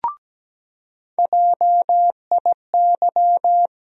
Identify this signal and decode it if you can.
Morse code, heard as a keyed tone, about 17 wpm, 700 Hz. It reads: JIY
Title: USA Call Sign Postfixes - 3 Letters 17wpm